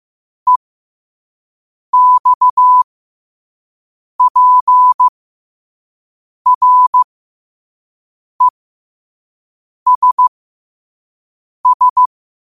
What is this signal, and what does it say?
Morse code, heard as a keyed tone, about 15 wpm, spaced as for 5 wpm, 1000 Hz. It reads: EXPRESS
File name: qotdfiles / 5wpm